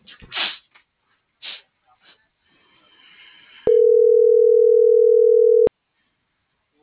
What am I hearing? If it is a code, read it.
Ringback tone (North America), 440 + 480 Hz.